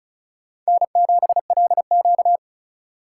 Morse code, heard as a keyed tone, about 36 wpm, 700 Hz. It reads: N7LQ
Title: USA Call Signs - 2x Character Spacing - 35wpm